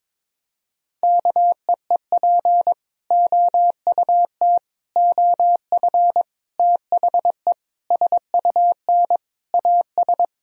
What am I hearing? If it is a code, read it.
Morse code, heard as a keyed tone, about 22 wpm, 700 Hz. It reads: KEEP OUT OF THE SUN AS
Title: Sentences from Top 1000 Words - Review Part 1 - 22wpm